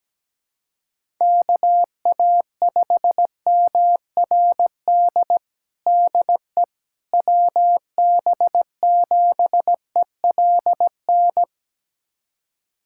Morse code, heard as a keyed tone, about 17 wpm, 700 Hz. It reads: KA5MRD DE WB7ELN
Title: USA Callsign DE Callsign 17wpm